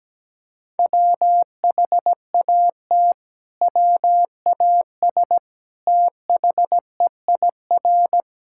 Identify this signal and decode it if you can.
Morse code, heard as a keyed tone, about 17 wpm, 700 Hz. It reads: WHAT WAS THEIR
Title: Sentences from Top 1000 Words - Review Part 6 - 17wpm